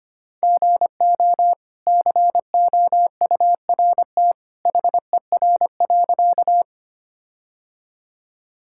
Morse code, heard as a keyed tone, about 25 wpm, 700 Hz. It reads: GO COURT HER.
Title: Sentences from Top 1600 Words 25wpm